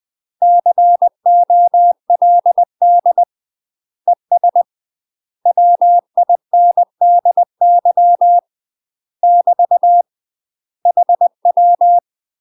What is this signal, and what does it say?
Morse code, heard as a keyed tone, about 20 wpm, 700 Hz. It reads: COLD ES WINDY <BT> HW
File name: UK MASTER QSOs TRAINING / 20WPM / UK MASTER QSOs 0047